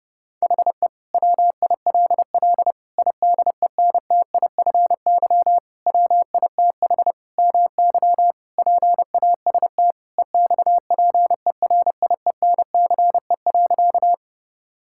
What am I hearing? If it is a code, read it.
Morse code, heard as a keyed tone, about 30 wpm, 700 Hz. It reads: HE WILL IDENTIFY WITH MY PAST EXPERIENCE.
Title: Sentences from Top 1000 Words - Review Part 6 - 30wpm